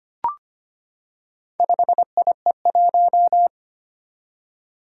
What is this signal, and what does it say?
Morse code, heard as a keyed tone, about 25 wpm, 700 Hz. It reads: 5IE1
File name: practice / 25wpm / Four Letter-Number Combinations - 25wpm